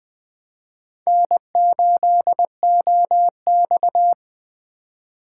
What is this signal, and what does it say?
Morse code, heard as a keyed tone, about 20 wpm, 700 Hz. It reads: N8OX
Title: USA Call Signs - 4 Characters - 2x Character Spacing - 20wpm